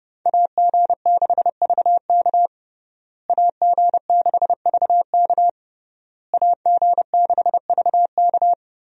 Morse code, heard as a keyed tone, about 30 wpm, 700 Hz. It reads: AG6VK AG6VK AG6VK
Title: USA Call Signs - 5 Characters - Repeated 3x - 30wpm